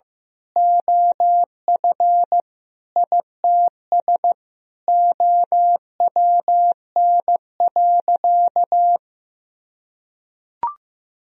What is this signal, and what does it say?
Morse code, heard as a keyed tone, about 16 wpm, 700 Hz. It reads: OF ITS OWN.
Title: Sentences from Top 1000 Words - Review Part 4 - 15wpm